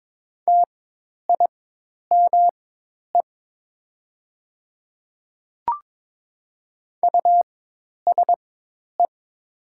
Morse code, heard as a keyed tone, about 22 wpm, spaced as for 10 wpm, 700 Hz. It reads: TIME USE
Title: 4x Character Spacing - Top 100 Words - 22wpm